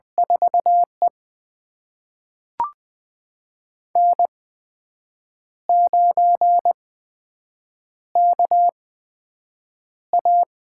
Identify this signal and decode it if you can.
Morse code, heard as a keyed tone, about 20 wpm, spaced as for 7 wpm, 700 Hz. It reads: 4E N9KA
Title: USA Call Signs - 4 Characters - 8x Character Spacing - 20wpm